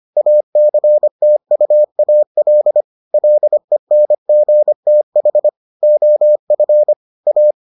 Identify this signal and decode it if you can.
Morse code, heard as a keyed tone, about 25 wpm, 600 Hz. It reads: ACTUAL LENGTH OF A